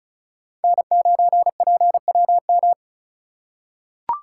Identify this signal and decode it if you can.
Morse code, heard as a keyed tone, about 36 wpm, 700 Hz. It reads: N9PWM
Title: USA Call Signs - 3x Character Spacing - 35wpm